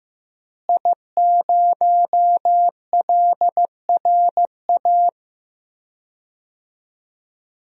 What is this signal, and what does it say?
Morse code, heard as a keyed tone, about 15 wpm, 700 Hz. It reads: I0LRA
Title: DXCC-100 Call Signs - 5 Characters - Repeated 3x - 15wpm